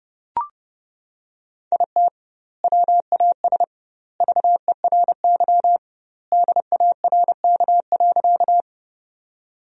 Morse code, heard as a keyed tone, about 30 wpm, 700 Hz. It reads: IT WAS VERY DARK.
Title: 2x Word-Spacing - Sentences from Top 500 Words - 30wpm